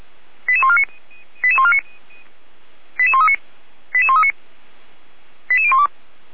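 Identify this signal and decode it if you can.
Five-tone selective call (ZVEI), heard as quick runs of five tones, tones of about 70 ms. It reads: 88149 88139 88139 88012
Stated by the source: Five-tone ZVEI selective call received by SDR, recorded by Feuerwehr Orlamünde.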